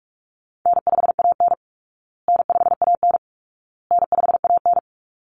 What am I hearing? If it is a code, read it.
Morse code, heard as a keyed tone, about 44 wpm, 700 Hz. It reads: N5AN N5AN N5AN
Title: Easy Call Signs Repeated 3x - 45wpm